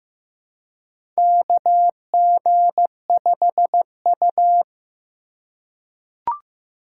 Morse code, heard as a keyed tone, about 15 wpm, 700 Hz. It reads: KG5U
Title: USA Call Signs - 4 Characters - 4x Character Spacing - 15wpm